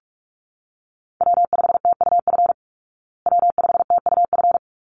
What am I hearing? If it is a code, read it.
Morse code, heard as a keyed tone, about 44 wpm, 700 Hz. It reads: W5TUF W5TUF
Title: Call Signs Repeated 3x - 45wpm